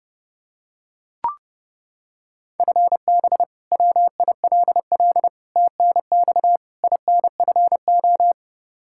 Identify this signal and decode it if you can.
Morse code, heard as a keyed tone, about 32 wpm, 700 Hz. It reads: FB WILL TNX INFO
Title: Practice QSO - Exchange 3 - More Info - 30wpm